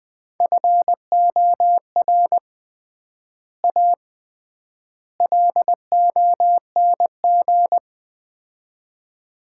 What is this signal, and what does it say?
Morse code, heard as a keyed tone, about 20 wpm, 700 Hz. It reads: FOR A LONG
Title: 3x Word-Spacing - Sets of 3 Words - 20wpm